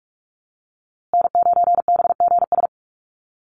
Morse code, heard as a keyed tone, about 44 wpm, 700 Hz. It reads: N9BGS
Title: USA Call Signs Repeated 3x 45wpm